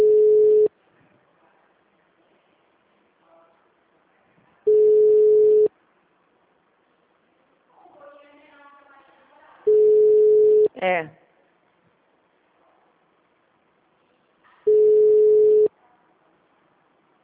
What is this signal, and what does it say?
Ringback tone (Europe, 425 Hz), with speech.
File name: out- -2014- - - .2867